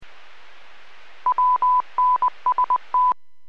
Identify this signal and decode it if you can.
Morse code, heard as a keyed tone, about 20 wpm, 1000 Hz. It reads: WNST
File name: WPS / 00266 17-02 WPS 451.25 WPS